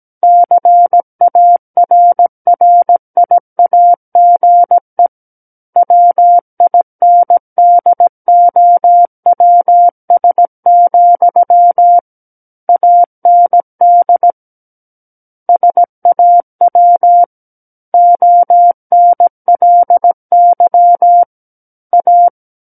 Morse code, heard as a keyed tone, about 17 wpm, 700 Hz. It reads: CARRIAGE WINDOWS, AND SAW ONLY A
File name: Book One / 08-Chapter 8 - 17 WPM